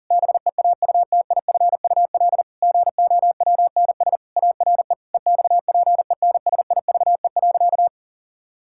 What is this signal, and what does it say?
Morse code, heard as a keyed tone, about 40 wpm, 700 Hz. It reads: BEAUTIFUL GOWNS ARE EXPENSIVE.
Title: Pangram Challenge v2 - 40wpm